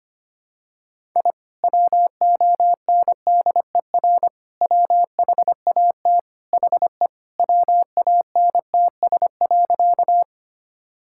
Morse code, heard as a keyed tone, about 25 wpm, 700 Hz. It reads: I WONDER WHAT HE WANTS.
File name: practice / 25wpm / Sentences with 2-Word Phrases - 25wpm